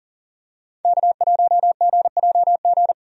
Morse code, heard as a keyed tone, about 40 wpm, 700 Hz. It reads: K1GJG
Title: USA Call Signs - 5 Characters - 40wpm